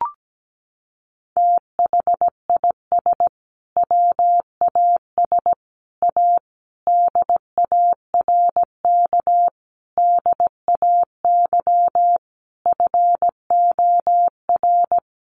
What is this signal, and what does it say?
Morse code, heard as a keyed tone, about 17 wpm, 700 Hz. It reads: THIS WAS A DARK DAY FOR
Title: Sentences from Top 1300 Words 17wpm